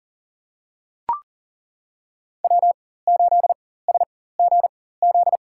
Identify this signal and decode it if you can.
Morse code, heard as a keyed tone, about 38 wpm, spaced as for 18 wpm, 700 Hz. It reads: W8SGZ
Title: USA Call Signs - 4x Character Spacing - 40wpm